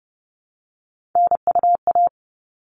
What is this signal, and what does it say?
Morse code, heard as a keyed tone, about 30 wpm, 700 Hz. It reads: NUA
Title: All Cut Numbers - 3-Digit 30wpm